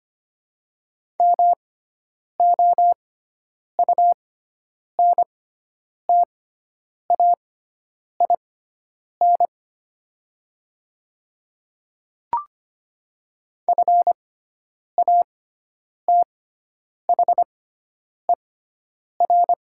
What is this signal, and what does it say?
Morse code, heard as a keyed tone, about 25 wpm, spaced as for 9 wpm, 700 Hz. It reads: MOUNTAIN FATHER
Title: 6x Character Spacing - Top 500 Words - 25wpm